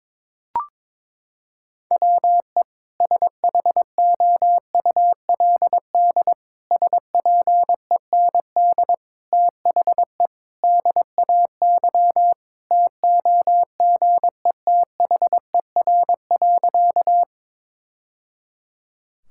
Morse code, heard as a keyed tone, about 22 wpm, 700 Hz. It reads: WE SHOULD SPEND THE DAY TOGETHER.
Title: Sentences from Top 1000 Words - Review Part 5 - 22wpm